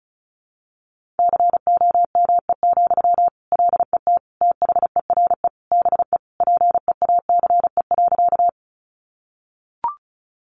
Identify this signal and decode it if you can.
Morse code, heard as a keyed tone, about 34 wpm, 700 Hz. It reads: COME, LET THERE BE PEACE.
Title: Sentences from Top 1500 Words 35wpm